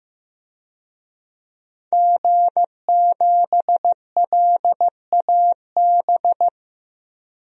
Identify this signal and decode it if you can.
Morse code, heard as a keyed tone, about 15 wpm, 700 Hz. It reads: G7LAB
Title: DXCC-100 Call Signs - 5 Characters - 15wpm